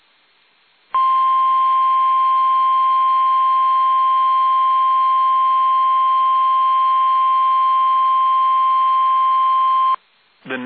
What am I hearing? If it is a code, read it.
Emergency Alert System (SAME) tones (the 1050 Hz attention tone), with speech. It is heard from 0.9 to 10.0 s.